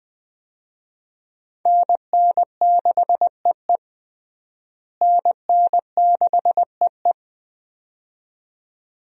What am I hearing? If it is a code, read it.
Morse code, heard as a keyed tone, about 21 wpm, 700 Hz. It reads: NN6EE NN6EE
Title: USA Call Signs - 5 Characters - Repeated 2x - 20wpm